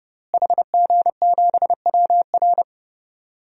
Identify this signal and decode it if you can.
Morse code, heard as a keyed tone, about 31 wpm, 700 Hz. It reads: HG7WR
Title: DXCC-100 Call Signs - 5 Characters - Repeated 2x - 30wpm